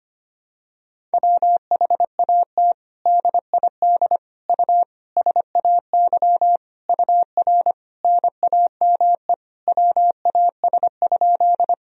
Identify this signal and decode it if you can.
Morse code, heard as a keyed tone, about 25 wpm, 700 Hz. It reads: WHAT DID U SAY UR NAME WAS?
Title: Sentences from Top 200 Words and Abbreviations 25wpm